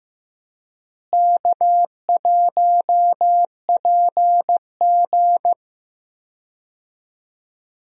Morse code, heard as a keyed tone, about 15 wpm, 700 Hz. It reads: K1PG
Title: USA Call Signs - 4 Characters - Repeated 3x - 15wpm